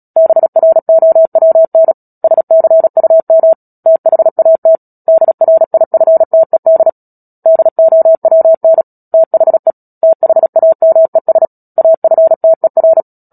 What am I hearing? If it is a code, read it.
Morse code, heard as a keyed tone, about 37 wpm, 650 Hz. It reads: BROWN SCUM THAT DRIFTED DOWN THE THAMES AFTER
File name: Book Two / 10-Chapter 10 - Epilogue - 36 WPM